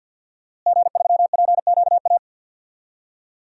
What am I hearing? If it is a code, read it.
Morse code, heard as a keyed tone, about 47 wpm, 700 Hz. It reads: G3PQA
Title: DXCC-100 Call Signs - 5 Characters - Repeated 2x - 50wpm